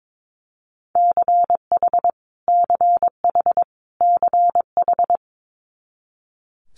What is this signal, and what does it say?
Morse code, heard as a keyed tone, about 22 wpm, 700 Hz. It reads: CH CH CH
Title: Multi-letter Phonemes Repeated 3x 22wpm